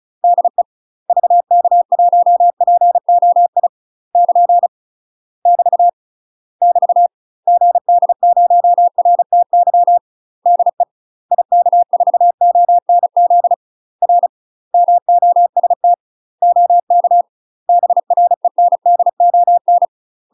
Morse code, heard as a keyed tone, about 35 wpm, 700 Hz. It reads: DE VK1POI <KN> <BT> <BT> GD0RTY DE IK4ONZ R MOST OK BRENDON
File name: UK MASTER QSOs TRAINING / 35WPM / UK MASTER QSOs 0003